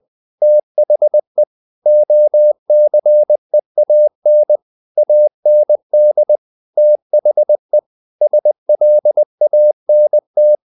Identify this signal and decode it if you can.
Morse code, heard as a keyed tone, about 20 wpm, 600 Hz. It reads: THE OCEAN AND THE SLANT
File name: the old man and the sea part 2 20wpm